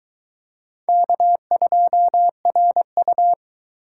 Morse code, heard as a keyed tone, about 23 wpm, 700 Hz. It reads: K2RU